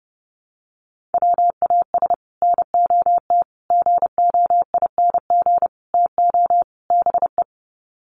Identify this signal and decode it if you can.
Morse code, heard as a keyed tone, about 30 wpm, 700 Hz. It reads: WAS NOT GOING TO BE
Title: Sets of 5 Words Encore 30wpm